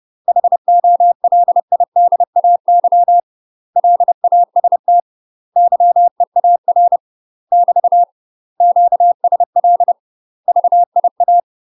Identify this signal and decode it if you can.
Morse code, heard as a keyed tone, about 30 wpm, 700 Hz. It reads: HOLIDAY LAST YEAR <BT> QSL VIA
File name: UK MASTER QSOs TRAINING / 30WPM / UK MASTER QSOs 0045